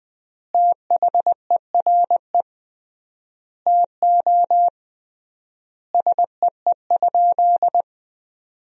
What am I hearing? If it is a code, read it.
Morse code, heard as a keyed tone, about 20 wpm, 700 Hz. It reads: THERE TO SEE?
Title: 3x Word-Spacing - Sentences from Top 100 Words - 20wpm